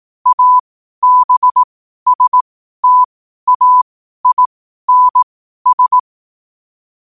Morse code, heard as a keyed tone, about 18 wpm, spaced as for 12 wpm, 1000 Hz. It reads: ABSTAINS
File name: qotdfiles / 12wpm